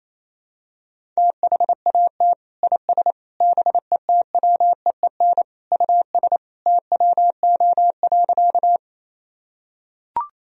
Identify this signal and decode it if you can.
Morse code, heard as a keyed tone, about 28 wpm, 700 Hz. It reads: THAT IS BETWEEN US TWO.
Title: Sentences from Top 1000 Words - Review Part 1 - 28wpm